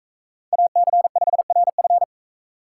Morse code, heard as a keyed tone, about 40 wpm, 700 Hz. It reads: AC5RF